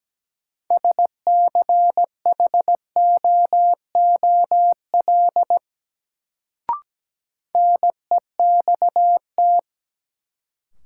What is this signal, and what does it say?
Morse code, heard as a keyed tone, about 17 wpm, 700 Hz. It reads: SCHOOL NEXT
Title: Sentences from Top 300 Words - New Word Drill - 17wpm